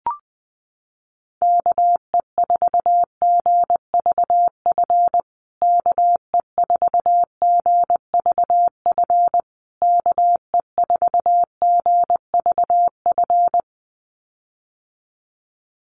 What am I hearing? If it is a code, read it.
Morse code, heard as a keyed tone, about 20 wpm, 700 Hz. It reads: KE4GVF KE4GVF KE4GVF
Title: USA Call Signs Repeated 3x 20wpm